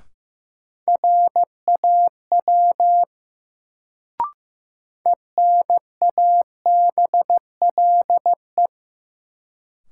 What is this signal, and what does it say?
Morse code, heard as a keyed tone, about 16 wpm, 700 Hz. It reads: RAW ENABLE
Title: Sentences from Top 1500 Words - New Word Drill - 15wpm